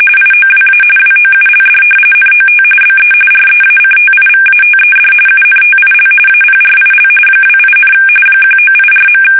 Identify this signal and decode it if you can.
RTTY, about 45.45 baud. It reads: IOWA WOULD NOT BE REACTIVATED UNTIL APRIL 28, 1984.
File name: fsk850 / battleshipnj